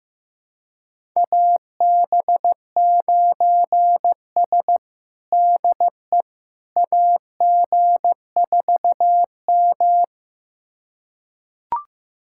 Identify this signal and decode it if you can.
Morse code, heard as a keyed tone, about 15 wpm, 700 Hz. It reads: AB9S DE AG4M
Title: USA Callsign DE Callsign - 4 Characters - 15wpm